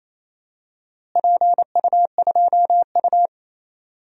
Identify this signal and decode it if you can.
Morse code, heard as a keyed tone, about 28 wpm, 700 Hz. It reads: PU2U
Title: DXCC-100 Call Signs - 4 Characters - Repeated 3x - 28wpm